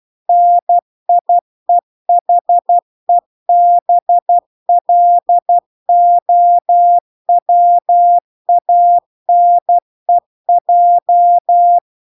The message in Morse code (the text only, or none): NIEHEBLOWANEJ